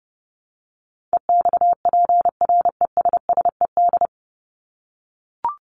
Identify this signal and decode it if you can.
Morse code, heard as a keyed tone, about 30 wpm, 700 Hz. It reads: EXPRESSED
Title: ED Suffix - Top 100 Words with Base Words 30wpm